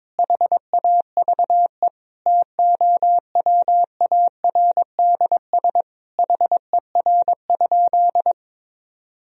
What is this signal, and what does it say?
Morse code, heard as a keyed tone, about 22 wpm, 700 Hz. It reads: HAVE TOWARDS HER?
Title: Sentences from Top 1000 Words - Review Part 4 - 22wpm